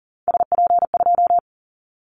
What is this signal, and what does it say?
Morse code, heard as a keyed tone, about 40 wpm, 700 Hz. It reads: SP2
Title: Callsign Prefixes for Top 25 DXCC - Repeated 3x - 40wpm